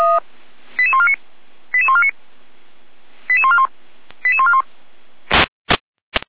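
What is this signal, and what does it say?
Five-tone selective call (ZVEI), heard as quick runs of five tones, tones of about 70 ms. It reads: 88149 88149 88141 88141
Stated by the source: Five-tone ZVEI selective call received on an SDR.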